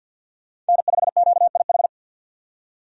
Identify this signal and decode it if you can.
Morse code, heard as a keyed tone, about 49 wpm, 700 Hz. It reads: N5QIH
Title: USA Call Signs - 5 Characters - 50wpm